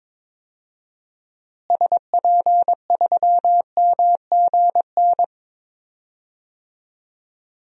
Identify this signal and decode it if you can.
Morse code, heard as a keyed tone, about 22 wpm, 700 Hz. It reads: SP3MGN
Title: DXCC-100 Call Signs - 6 Characters - Repeated 3x - 22wpm